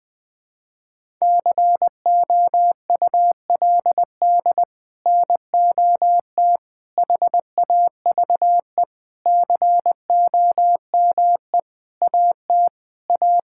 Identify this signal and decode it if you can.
Morse code, heard as a keyed tone, about 21 wpm, 700 Hz. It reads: COULD NOT HAVE COME AT A
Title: Sets of 8 Words - 20wpm